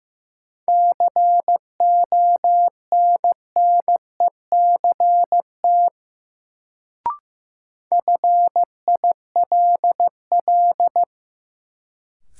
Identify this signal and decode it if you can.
Morse code, heard as a keyed tone, about 15 wpm, 700 Hz. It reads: CONNECT FILL
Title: Sentences from Top 1500 Words - New Word Drill Alt - 15wpm